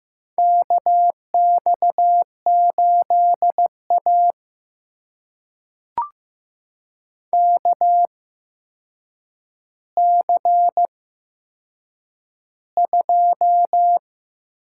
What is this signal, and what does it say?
Morse code, heard as a keyed tone, about 15 wpm, spaced as for 7 wpm, 700 Hz. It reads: KX8A KC2
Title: USA Call Signs - 4 Characters - 8x Character Spacing - 15wpm